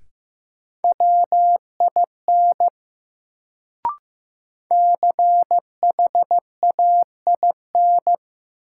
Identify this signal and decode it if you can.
Morse code, heard as a keyed tone, about 15 wpm, 700 Hz. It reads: WIN CHAIN
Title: Sentences from Top 1500 Words - New Word Drill Alt - 15wpm